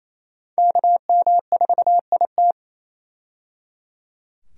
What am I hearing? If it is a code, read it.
Morse code, heard as a keyed tone, about 28 wpm, 700 Hz. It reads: KM4IT
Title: USA Call Signs - 5 Characters - 28wpm